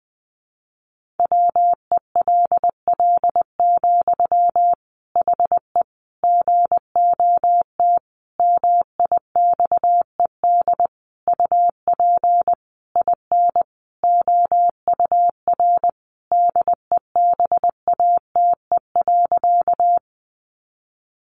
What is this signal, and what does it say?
Morse code, heard as a keyed tone, about 20 wpm, 700 Hz. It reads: WELL, HE GOT MIXED UP IN OUR DEBATE.
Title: Sentences from Top 1500 Words 20wpm